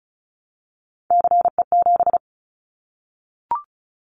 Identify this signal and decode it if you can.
Morse code, heard as a keyed tone, about 34 wpm, 700 Hz. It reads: CE7
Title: Callsign Prefixes for Top 100 DXCC - 35wpm